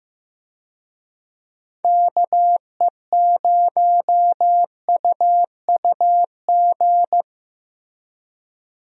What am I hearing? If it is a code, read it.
Morse code, heard as a keyed tone, about 14 wpm, 700 Hz. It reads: KE0UUG
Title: USA Call Signs - 6 Characters - Repeated 2x - 15wpm